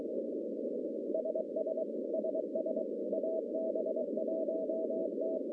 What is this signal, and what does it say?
Morse code, heard as a keyed tone, about 23 wpm, 640 Hz, with noise. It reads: SS SS AB1T
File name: contests / CW-SS-k5tr- -1100